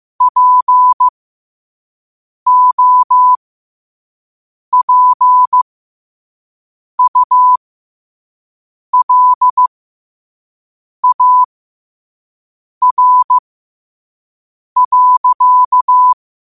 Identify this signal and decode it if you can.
Morse code, heard as a keyed tone, about 15 wpm, spaced as for 5 wpm, 1000 Hz. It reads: POPULAR.
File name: qotdfiles / 5wpm